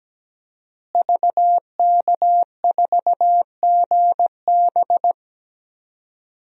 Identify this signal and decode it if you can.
Morse code, heard as a keyed tone, about 17 wpm, 700 Hz. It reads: VK4GB
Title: DXCC-100 Call Signs - 5 Characters - Repeated 2x - 17wpm